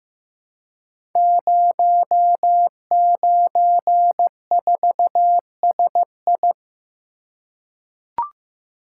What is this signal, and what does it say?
Morse code, heard as a keyed tone, about 15 wpm, 700 Hz. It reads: 094SI
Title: Code Groups - 5 Letter:Numbers - 15wpm